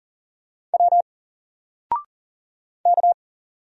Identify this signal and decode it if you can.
Morse code, heard as a keyed tone, about 40 wpm, 700 Hz. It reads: W K
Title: Single Character (Letter:Number:Punctuation:Prosign) Alt 40wpm